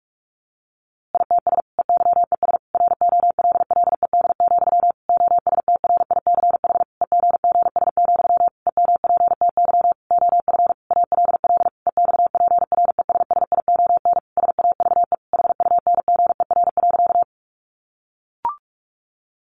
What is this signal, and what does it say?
Morse code, heard as a keyed tone, about 44 wpm, 700 Hz. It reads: ITS EYES ROLLED, OSTRICH EGGS, EMPTY OF ALL EXPRESSION SAVE HUNGER.
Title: A Sound Of Thunder 45wpm